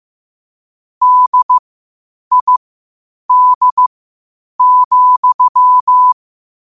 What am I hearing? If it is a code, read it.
Morse code, heard as a keyed tone, about 15 wpm, spaced as for 8 wpm, 1000 Hz. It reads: DID,
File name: qotdfiles / 8wpm